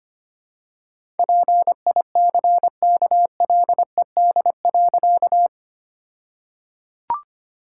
Morse code, heard as a keyed tone, about 25 wpm, 700 Hz. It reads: PICKLED.
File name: practice / 25wpm / All-Age Appropriate Jokes - 25wpm